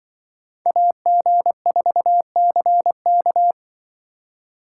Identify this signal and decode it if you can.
Morse code, heard as a keyed tone, about 24 wpm, 700 Hz. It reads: AG4CK
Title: Speed Racing - USA 2x2 Call Signs 20wpm